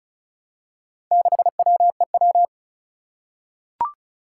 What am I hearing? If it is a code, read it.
Morse code, heard as a keyed tone, about 36 wpm, 700 Hz. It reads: 6WEW
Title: Four Challenging Letter-Number Combinations 35wpm